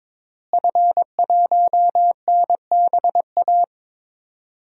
Morse code, heard as a keyed tone, about 22 wpm, 700 Hz. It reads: F1NBA
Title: DXCC-100 Call Signs - 5 Characters - Repeated 2x - 22wpm